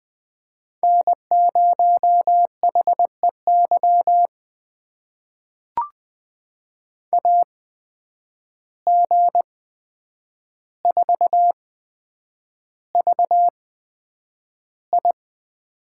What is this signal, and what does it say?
Morse code, heard as a keyed tone, about 20 wpm, spaced as for 9 wpm, 700 Hz. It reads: N0HEY AG4VI
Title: USA Call Signs - 5 Characters - 8x Character Spacing - 20wpm